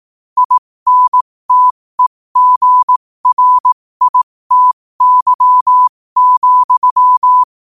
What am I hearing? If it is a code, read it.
Morse code, heard as a keyed tone, about 18 wpm, spaced as for 15 wpm, 1000 Hz. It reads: INTEGRITY,
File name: qotdfiles / 15wpm